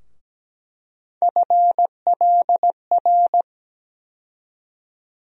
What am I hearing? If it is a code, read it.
Morse code, heard as a keyed tone, about 17 wpm, 700 Hz. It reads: FLR
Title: USA Call Sign Postfixes - 3 Letters 17wpm